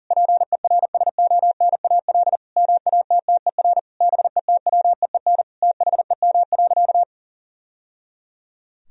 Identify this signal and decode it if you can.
Morse code, heard as a keyed tone, about 37 wpm, 700 Hz. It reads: PERSONAL MATTER BETWEEN THEM.
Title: Sentences from Top 1000 Words - Review Part 4 - 40wpm